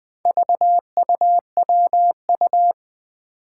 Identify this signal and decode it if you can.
Morse code, heard as a keyed tone, about 20 wpm, 700 Hz. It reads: VUWU
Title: Four Challenging Letter-Number Combinations 20wpm